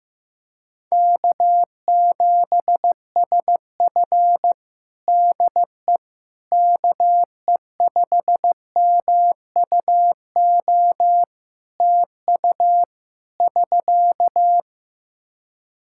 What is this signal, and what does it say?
Morse code, heard as a keyed tone, about 16 wpm, 700 Hz. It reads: K7SF DE KE5MUO TU <SK>
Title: K1USN SST QSO Practice 15wpm